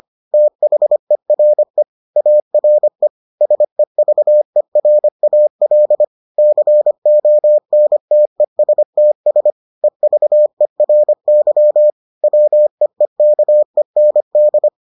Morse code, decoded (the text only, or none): THERE ARE SEVERAL CONTESTS EVERY WEEKEND